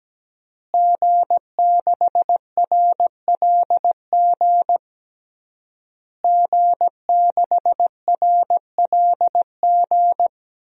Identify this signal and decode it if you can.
Morse code, heard as a keyed tone, about 17 wpm, 700 Hz. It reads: G6RLG G6RLG
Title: DXCC-100 Call Signs - 5 Characters - Repeated 3x - 17wpm